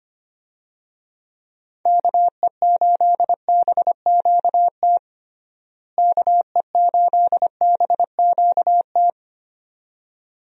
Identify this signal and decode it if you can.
Morse code, heard as a keyed tone, about 25 wpm, 700 Hz. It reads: KE8BQT KE8BQT
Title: USA Call Signs - 6 Characters - Repeated 2x - 25wpm